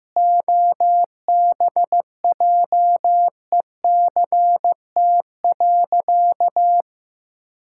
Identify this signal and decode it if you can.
Morse code, heard as a keyed tone, about 14 wpm, 700 Hz. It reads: OBJECT.
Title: Sentences from Top 1000 Words - Review Part 5 - 15wpm